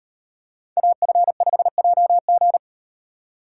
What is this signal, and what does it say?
Morse code, heard as a keyed tone, about 38 wpm, 700 Hz. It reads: AF5JG